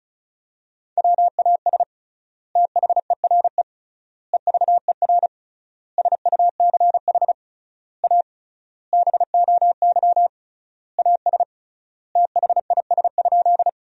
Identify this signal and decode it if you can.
Morse code, heard as a keyed tone, about 36 wpm, 700 Hz. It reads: WAS THERE EVER SUCH A BOY AS THIS?
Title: 3x Word-Spacing - Sentences from Top 400 Words - 35wpm